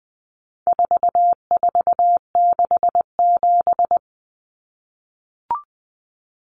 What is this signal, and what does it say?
Morse code, heard as a keyed tone, about 20 wpm, 700 Hz. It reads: 4467
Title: Numbers - 4-Digit 20wpm